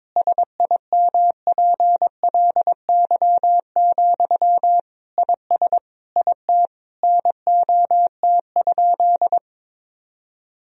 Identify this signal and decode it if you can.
Morse code, heard as a keyed tone, about 22 wpm, 700 Hz. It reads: SIMPLY, IS IT NOT?
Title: Sentences from Top 1000 Words - Review Part 4 - 22wpm